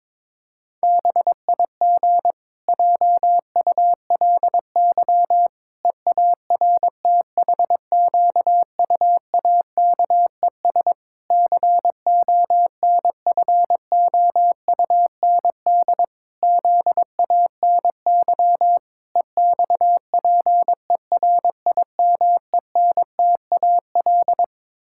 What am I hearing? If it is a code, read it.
Morse code, heard as a keyed tone, about 22 wpm, 700 Hz. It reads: BIG JULY EARTHQUAKES CONFOUND ZANY EXPERIMENTAL
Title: Pangram Challenge v2 - 22wpm